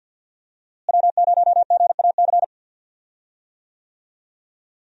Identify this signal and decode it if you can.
Morse code, heard as a keyed tone, about 52 wpm, 700 Hz. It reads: W0GAC
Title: USA Call Signs - 5 Characters - Repeated 2x - 50wpm